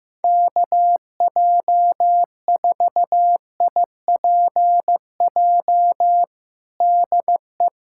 Morse code, heard as a keyed tone, about 15 wpm, 700 Hz. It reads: KJ4IPJ DE
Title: USA Callsign DE Callsign - 6 Characters - 15wpm